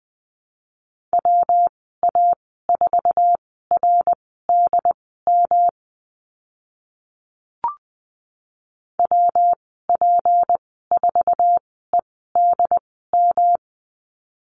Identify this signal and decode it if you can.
Morse code, heard as a keyed tone, about 20 wpm, spaced as for 15 wpm, 700 Hz. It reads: WA4RDM WP4EDM
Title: 186 Lesson 30g - Letter b - Call Signs 20wpm